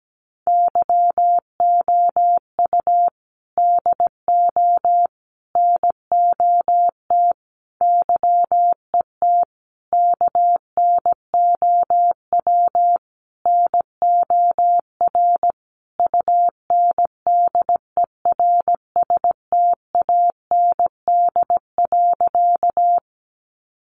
Morse code, heard as a keyed tone, about 17 wpm, 700 Hz. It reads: YOU DO NOT YET KNOW NOR UNDERSTAND.
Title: Sentences from Top 1500 Words 17wpm